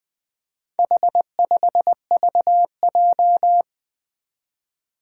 Morse code, heard as a keyed tone, about 20 wpm, 700 Hz. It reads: H5VJ
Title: Four Challenging Letter-Number Combinations 20wpm